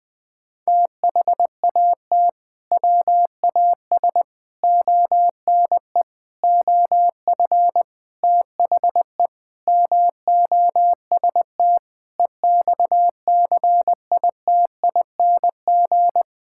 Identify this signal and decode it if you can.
Morse code, heard as a keyed tone, about 20 wpm, 700 Hz. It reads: THAT WAS ONE OF THE MOST EXCITING
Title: Sentences with 5-Word Phrases - 20wpm